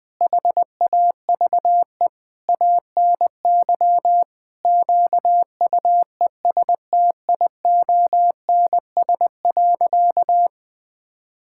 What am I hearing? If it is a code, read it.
Morse code, heard as a keyed tone, about 21 wpm, 700 Hz. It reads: HAVE ANY QUESTIONS.
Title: Sentences with 8-Word Phrases - 20wpm